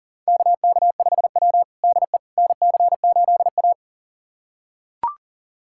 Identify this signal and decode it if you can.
Morse code, heard as a keyed tone, about 40 wpm, 700 Hz. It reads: KK5W DE NC8A
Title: USA Callsign DE Callsign - 4 Characters - 40wpm